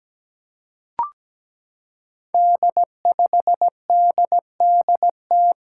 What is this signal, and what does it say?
Morse code, heard as a keyed tone, about 17 wpm, 700 Hz. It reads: D5DDT
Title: Five Challenging Letter-Number Combinations 17wpm